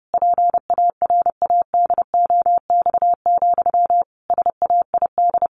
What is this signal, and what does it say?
Morse code, heard as a keyed tone, about 30 wpm, 700 Hz. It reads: PARADOX, SAID
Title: A Sound Of Thunder 30wpm